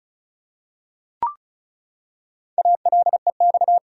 Morse code, heard as a keyed tone, about 34 wpm, 700 Hz. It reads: ALEX
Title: USA Common Names - Max 4 Letters - 35wpm